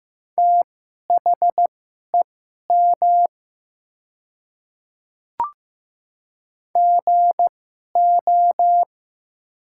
Morse code, heard as a keyed tone, about 15 wpm, spaced as for 11 wpm, 700 Hz. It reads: THEM GO
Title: 2x Character Spacing - Top 100 Words - 15wpm